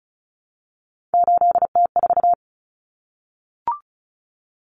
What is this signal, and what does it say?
Morse code, heard as a keyed tone, about 34 wpm, 700 Hz. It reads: 8T4
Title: Common Cut Numbers - 3-Digit 35wpm